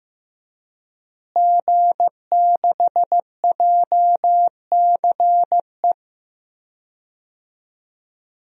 Morse code, heard as a keyed tone, about 16 wpm, 700 Hz. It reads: G6JCE
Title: DXCC-100 Call Signs - 5 Characters - Repeated 2x - 15wpm